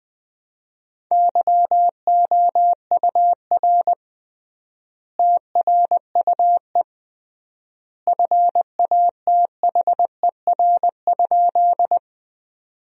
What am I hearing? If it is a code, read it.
Morse code, heard as a keyed tone, about 21 wpm, 700 Hz. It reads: YOUR TRUE FATHER?
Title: 3x Word-Spacing - Sentences from Top 400 Words - 20wpm